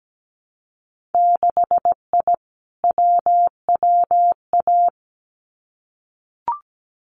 Morse code, heard as a keyed tone, about 17 wpm, 700 Hz. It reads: 6I WWA
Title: Winter Field Day Exchanges Encore 17wpm